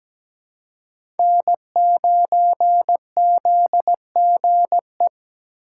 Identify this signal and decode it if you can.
Morse code, heard as a keyed tone, about 17 wpm, 700 Hz. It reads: N9ZGE
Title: USA Call Signs - 6x Character Spacing - 17wpm